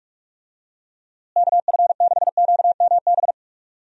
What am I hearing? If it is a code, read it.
Morse code, heard as a keyed tone, about 44 wpm, 700 Hz. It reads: KF6QMB
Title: USA Call Signs - 6 Characters - Repeated 3x - 45wpm